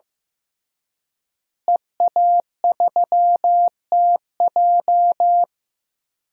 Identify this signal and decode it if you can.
Morse code, heard as a keyed tone, about 16 wpm, 700 Hz. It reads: EA3TJ
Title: DXCC-100 Call Signs - 5 Characters - Repeated 3x - 15wpm